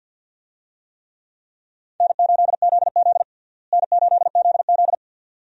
Morse code, heard as a keyed tone, about 52 wpm, 700 Hz. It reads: N8ZZ N8ZZ
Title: USA Call Signs - 4 Characters - Repeated 2x - 50wpm